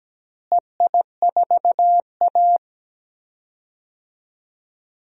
Morse code, heard as a keyed tone, about 17 wpm, 700 Hz. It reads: EI4A
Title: DXCC-100 Call Signs - 4 Characters - Repeated 3x - 17wpm